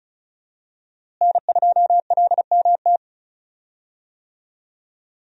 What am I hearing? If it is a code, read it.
Morse code, heard as a keyed tone, about 36 wpm, 700 Hz. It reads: N2LMT
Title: USA Call Signs - 5 Characters - Repeated 3x - 35wpm